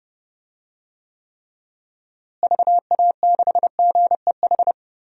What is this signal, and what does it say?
Morse code, heard as a keyed tone, about 30 wpm, 700 Hz. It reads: VA6GEH
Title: DXCC-100 Call Signs - 6 Characters - Repeated 3x - 30wpm